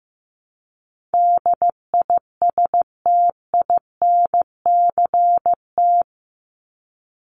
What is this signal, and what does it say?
Morse code, heard as a keyed tone, about 14 wpm, 700 Hz. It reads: DISTINCT
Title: Top 1501-1750 Words 15wpm